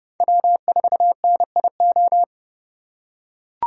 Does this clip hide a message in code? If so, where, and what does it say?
Yes: Morse code from 0.2 to 2.2 s, heard as a keyed tone, about 27 wpm, 700 Hz. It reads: W4NIO